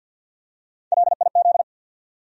Morse code, heard as a keyed tone, about 53 wpm, 700 Hz. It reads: LIZ
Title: USA Common Names - Max 5 Letters - 50wpm